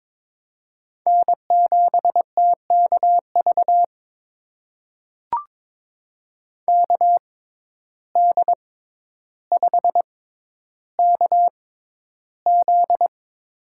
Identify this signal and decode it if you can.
Morse code, heard as a keyed tone, about 22 wpm, spaced as for 11 wpm, 700 Hz. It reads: N7TKV KD5KZ
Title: USA Call Signs - 5 Characters - 6x Character Spacing - 22wpm